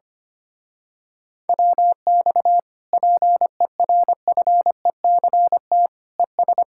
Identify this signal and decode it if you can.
Morse code, heard as a keyed tone, about 24 wpm, 700 Hz. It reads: WX PERFECT ES
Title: Speed-Racing Common QSO Phrases - 25wpm